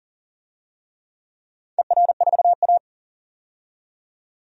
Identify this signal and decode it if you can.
Morse code, heard as a keyed tone, about 42 wpm, 700 Hz. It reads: ER4A
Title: DXCC-100 Call Signs - 4 Characters - 40wpm